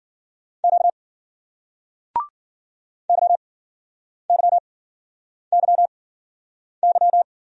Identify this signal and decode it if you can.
Morse code, heard as a keyed tone, about 53 wpm, spaced as for 9 wpm, 700 Hz. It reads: Q YYYY